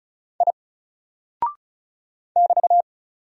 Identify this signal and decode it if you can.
Morse code, heard as a keyed tone, about 35 wpm, 700 Hz. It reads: I <BT>
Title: Single Character (Letter:Number:Punctuation:Prosign) 35wpm